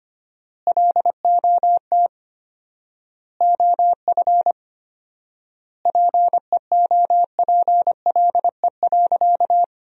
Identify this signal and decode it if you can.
Morse code, heard as a keyed tone, about 25 wpm, 700 Hz. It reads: LOT OF PEOPLE.
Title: 4x Word-Spacing - Sentences from Top 500 Words - 25wpm